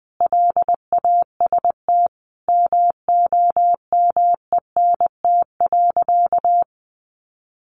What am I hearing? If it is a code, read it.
Morse code, heard as a keyed tone, about 20 wpm, 700 Hz. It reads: LAST MOMENT.
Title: Sentences from Top 1000 Words 20wpm